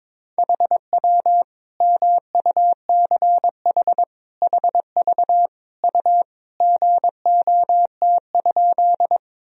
Morse code, heard as a keyed tone, about 22 wpm, 700 Hz. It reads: HW MUCH HV U GOT?
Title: Sentences from Top 300 Words and Abbreviations 22wpm